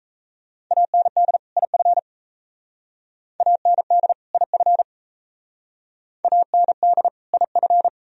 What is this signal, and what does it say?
Morse code, heard as a keyed tone, about 34 wpm, 690 Hz. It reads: AND IF AND IF AND IF
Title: Speed-Racing - Sets of 2 Words 28wpm